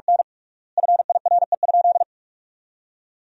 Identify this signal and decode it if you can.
Morse code, heard as a keyed tone, about 46 wpm, 700 Hz. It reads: N FIRE?
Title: 3x Word-Spacing - Sentences from Top 400 Words - 45wpm